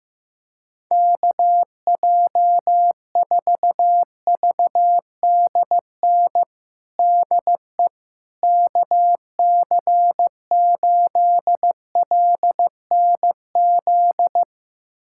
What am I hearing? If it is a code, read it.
Morse code, heard as a keyed tone, about 16 wpm, 700 Hz. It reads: KJ4VDN DE KC8LNZ
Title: USA Callsign DE Callsign - 6 Characters - 15wpm